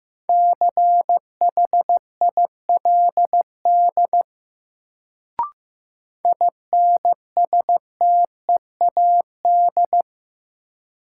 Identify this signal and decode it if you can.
Morse code, heard as a keyed tone, about 15 wpm, 700 Hz. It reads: CHILD INSTEAD
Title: Sentences from Top 800 Words - New Word Drill - 15wpm